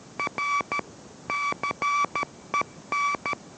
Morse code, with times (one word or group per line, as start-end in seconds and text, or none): 0.2-0.8 R
1.3-3.3 CEN